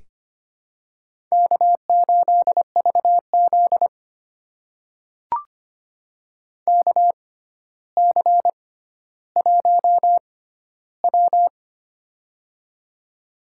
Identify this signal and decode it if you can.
Morse code, heard as a keyed tone, about 25 wpm, spaced as for 13 wpm, 700 Hz. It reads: K8VZ KC1W
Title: USA Call Signs - 4 Characters - 6x Character Spacing - 25wpm